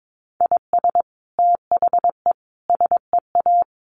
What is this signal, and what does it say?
Morse code, heard as a keyed tone, about 22 wpm, 700 Hz. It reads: IS THE SEA
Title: Sentences from Top 1400 Words 22wpm